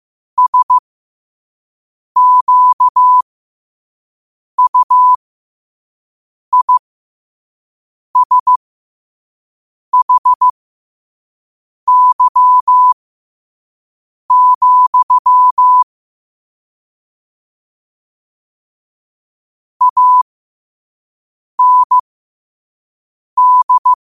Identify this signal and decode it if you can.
Morse code, heard as a keyed tone, about 15 wpm, spaced as for 5 wpm, 1000 Hz. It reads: SQUISHY, AND
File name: qotdfiles / 5wpm